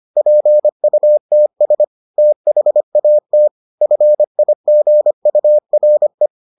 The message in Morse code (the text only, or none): PUTS THAT FIGURE